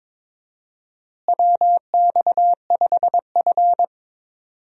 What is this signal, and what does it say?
Morse code, heard as a keyed tone, about 22 wpm, 700 Hz. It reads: WX5F
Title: USA Call Signs - 4 Characters - Repeated 3x - 22wpm